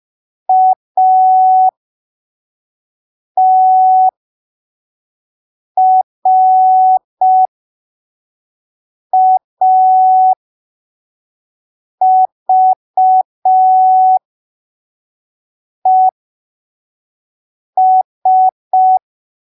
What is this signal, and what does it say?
Morse code, heard as a keyed tone, about 5 wpm, spaced as for 3 wpm, 750 Hz. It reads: ATRAVES
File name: arquivos / texto / texto 17 05ppm